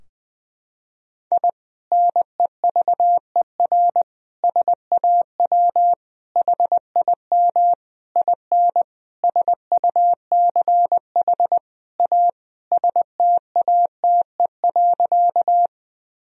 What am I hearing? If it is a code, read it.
Morse code, heard as a keyed tone, about 21 wpm, 700 Hz. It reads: I NEVER SAW HIM IN SUCH A STATE.
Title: Sentences from Top 1000 Words - Review Part 2 - 20wpm